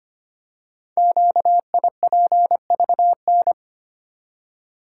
Morse code, heard as a keyed tone, about 25 wpm, 700 Hz. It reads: QIPVN
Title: Code Groups - 5 Letter:Numbers - 25wpm